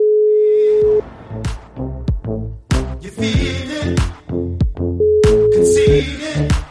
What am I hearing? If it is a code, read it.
Ringback tone (Europe, 425 Hz).